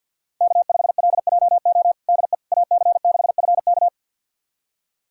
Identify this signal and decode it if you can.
Morse code, heard as a keyed tone, about 52 wpm, 690 Hz. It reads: K5LJO DE AC6FK
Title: USA Callsign DE Callsign - 5 Characters - 50wpm